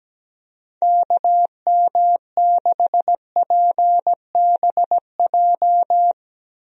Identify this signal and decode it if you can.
Morse code, heard as a keyed tone, about 17 wpm, 700 Hz. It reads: KM6PBJ
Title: USA Call Signs - 6 Characters - Repeated 2x - 17wpm